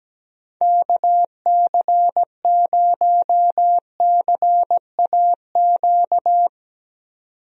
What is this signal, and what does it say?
Morse code, heard as a keyed tone, about 17 wpm, 700 Hz. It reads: KC0CAQ
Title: USA Call Signs - 6 Characters - 17wpm